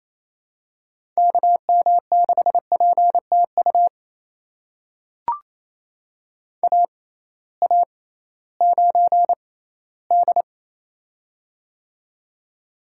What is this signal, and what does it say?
Morse code, heard as a keyed tone, about 28 wpm, spaced as for 16 wpm, 700 Hz. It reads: KM6PTU AA9D
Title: USA Call Signs - 6x Character Spacing - 28wpm